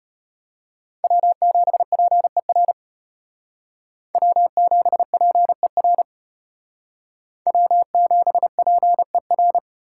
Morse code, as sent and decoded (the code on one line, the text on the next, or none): .-- --... .--. .-. / .-- --... .--. .-. / .-- --... .--. .-.
W7PER W7PER W7PER